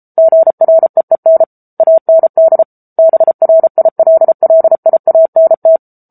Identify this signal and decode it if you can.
Morse code, heard as a keyed tone, about 34 wpm, 650 Hz. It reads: GREEN AND BRILLIANT
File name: Book Two / 07-Chapter 7 - 33 WPM